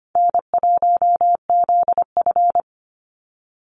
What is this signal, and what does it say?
Morse code, heard as a keyed tone, about 25 wpm, 700 Hz. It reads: N1ZF
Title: NA SOTA Chaser Callsigns 25wpm